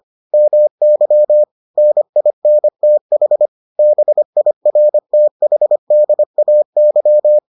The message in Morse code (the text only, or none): MY NINTH BIRTHDAY